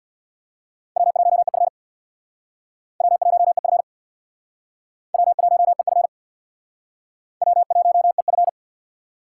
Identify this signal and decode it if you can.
Morse code, heard as a keyed tone, about 65 wpm, 700 Hz. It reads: W1EF W1EF W1EF W1EF